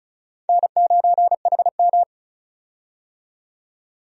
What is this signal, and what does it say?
Morse code, heard as a keyed tone, about 34 wpm, 700 Hz. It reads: N9HM
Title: USA Call Signs - 4 Characters - 35wpm